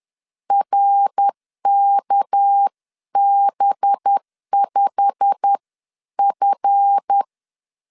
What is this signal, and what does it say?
Morse code, heard as a keyed tone, about 10 wpm, 800 Hz. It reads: RKB5F
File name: LEZIONI / 10 SIGNS / 10B INT 10wpm 2844